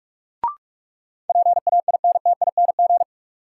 Morse code, heard as a keyed tone, about 46 wpm, 700 Hz. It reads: PAINTING
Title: Sentences from Top 1200 Words - New Word Drill Alt - 45wpm